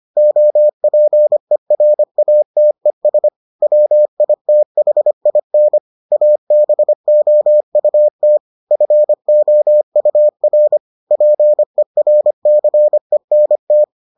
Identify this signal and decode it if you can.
Morse code, heard as a keyed tone, about 25 wpm, 600 Hz. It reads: OPERATES WITHIN ABOUT FOUR PERCENT